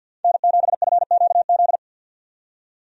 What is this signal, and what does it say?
Morse code, heard as a keyed tone, about 50 wpm, 700 Hz. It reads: N7FQZ
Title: USA Call Signs - 5 Characters - 50wpm